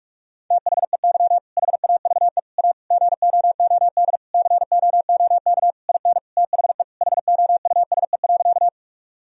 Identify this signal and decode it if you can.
Morse code, heard as a keyed tone, about 46 wpm, 700 Hz. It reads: THEY HAVE A GOOD COOK IN THE HOUSE.
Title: Sentences from Top 1000 Words - Review Part 4 - 45wpm